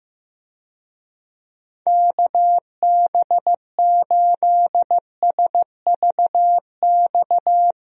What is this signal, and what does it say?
Morse code, heard as a keyed tone, about 15 wpm, 700 Hz. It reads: KB8SVX
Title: USA Call Signs - 6 Characters - Repeated 2x - 15wpm